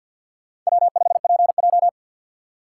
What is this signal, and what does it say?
Morse code, heard as a keyed tone, about 64 wpm, 700 Hz. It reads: W5PJ